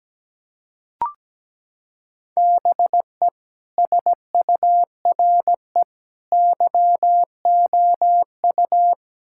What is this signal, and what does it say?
Morse code, heard as a keyed tone, about 17 wpm, 700 Hz. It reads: BE SURE YOU
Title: Sentences from Top 1600 Words 17wpm